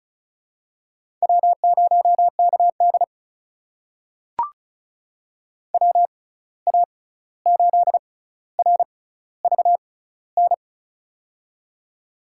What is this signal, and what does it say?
Morse code, heard as a keyed tone, about 35 wpm, spaced as for 16 wpm, 700 Hz. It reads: W0KD WA8RVN
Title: USA Call Signs - 6x Character Spacing - 35wpm